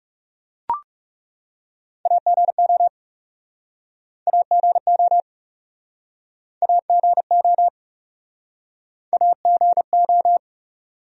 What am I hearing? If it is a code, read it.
Morse code, heard as a keyed tone, about 37 wpm, 700 Hz. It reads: AGO AGO AGO AGO
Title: Speed-Racing Top 500 Words 30wpm